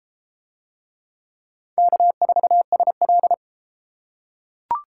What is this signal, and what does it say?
Morse code, heard as a keyed tone, about 31 wpm, 700 Hz. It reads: K4SL